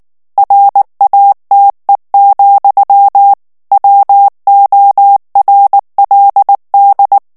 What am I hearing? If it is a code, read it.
Morse code, heard as a keyed tone, about 19 wpm, 800 Hz. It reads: RATE, WORLD
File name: morse mp3 / ANGLAIS texte clair 3 vitesse 18 mots minute